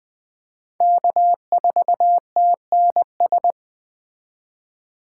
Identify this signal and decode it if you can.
Morse code, heard as a keyed tone, about 20 wpm, 700 Hz. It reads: K4TNS
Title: Speed Racing - USA 1x3 Call Signs 17wpm